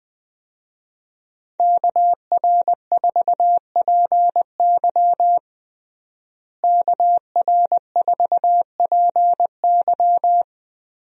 Morse code, heard as a keyed tone, about 20 wpm, 700 Hz. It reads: KR4PY KR4PY
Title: USA Call Signs - 5 Characters - Repeated 2x - 20wpm